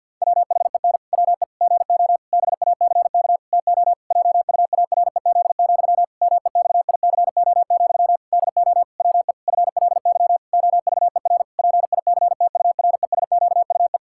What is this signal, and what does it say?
Morse code, heard as a keyed tone, about 47 wpm, 700 Hz. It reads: WHEN WE GO BACK TO JUAREZ, MEXICO, DO WE FLY OVER PICTURESQUE
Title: Pangram Challenge v2 - 50wpm